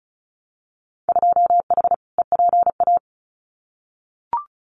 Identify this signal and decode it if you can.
Morse code, heard as a keyed tone, about 34 wpm, 700 Hz. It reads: 2H EPA
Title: Winter Field Day Exchanges Encore 35wpm